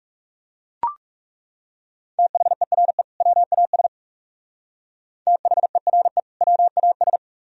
Morse code, heard as a keyed tone, about 37 wpm, 700 Hz. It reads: THERE WAS THERE WAS
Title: Speed-Racing - Sets of 2 Words 30wpm